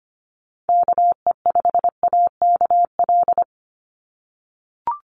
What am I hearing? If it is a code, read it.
Morse code, heard as a keyed tone, about 25 wpm, 700 Hz. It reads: KE5AKL
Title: NA SOTA Chaser Callsigns 25wpm